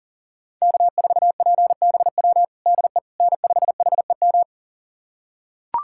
Morse code, heard as a keyed tone, about 40 wpm, 700 Hz. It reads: K4PBW DE N5HEM
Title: USA Callsign DE Callsign - 5 Characters - 40wpm